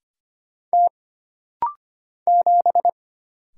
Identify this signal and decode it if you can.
Morse code, heard as a keyed tone, about 25 wpm, 700 Hz. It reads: T 7
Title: Single Letter-Number - Rapid-Fire 25wpm